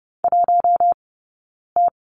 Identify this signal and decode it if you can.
Morse code, heard as a keyed tone, about 30 wpm, spaced as for 9 wpm, 690 Hz. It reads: 1T
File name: Lesson 17d - Letter u - Two Characters 30wpm